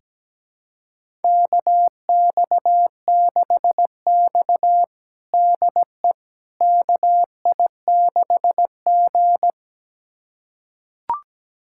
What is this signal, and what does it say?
Morse code, heard as a keyed tone, about 17 wpm, 700 Hz. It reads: KX6X DE KI6G
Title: USA Callsign DE Callsign - 4 Characters - 17wpm